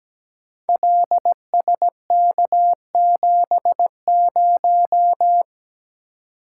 Morse code, heard as a keyed tone, about 17 wpm, 700 Hz. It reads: LSK70
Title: Code Groups - 5 Letter:Numbers - 17wpm